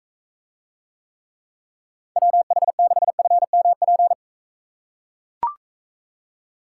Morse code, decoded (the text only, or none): WH6FMP